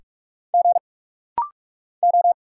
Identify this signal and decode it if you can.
Morse code, heard as a keyed tone, about 44 wpm, 700 Hz. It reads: G O
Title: Single Letter-Number Alt - Rapid-Fire 45wpm